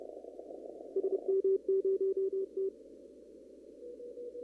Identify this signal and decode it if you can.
Morse code, heard as a keyed tone, about 30 wpm, 390 Hz. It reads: SM0T
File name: CW / kt5j wpx cw-op-k5pi